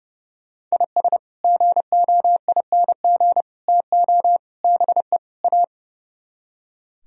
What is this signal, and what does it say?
Morse code, heard as a keyed tone, about 37 wpm, 700 Hz. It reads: IS GOING TO BE A
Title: Speed-Racing - Sets of 5 Words 30wpm